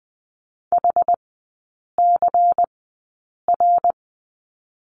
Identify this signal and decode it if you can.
Morse code, heard as a keyed tone, about 20 wpm, spaced as for 8 wpm, 700 Hz. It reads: HCR
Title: 107 Lesson 18e - Letter m - Three Characters 20wpm